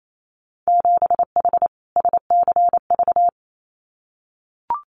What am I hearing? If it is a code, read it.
Morse code, heard as a keyed tone, about 28 wpm, 700 Hz. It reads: 7H SCV
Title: Winter Field Day Exchanges Encore 28wpm